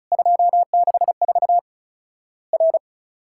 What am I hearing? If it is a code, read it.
Morse code, heard as a keyed tone, about 36 wpm, 700 Hz. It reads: 264 R
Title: Worked All Europe DX Contest - QTC Exchange - 35wpm